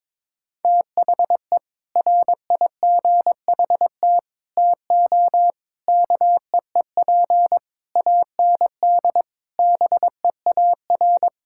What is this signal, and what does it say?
Morse code, heard as a keyed tone, about 22 wpm, 700 Hz. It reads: THE RIGHT TO KEEP AND BEAR
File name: practice / 22wpm / Sets of 7 Words - 22wpm